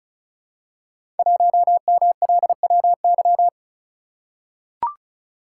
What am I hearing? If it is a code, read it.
Morse code, heard as a keyed tone, about 36 wpm, 700 Hz. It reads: 1MLWY
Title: Code Groups - 5 Letter:Numbers - 35wpm